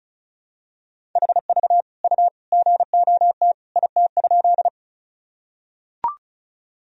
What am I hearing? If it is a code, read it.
Morse code, heard as a keyed tone, about 36 wpm, 700 Hz. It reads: HV U GOT IT?
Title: Sentences from Top 300 Words and Abbreviations 35wpm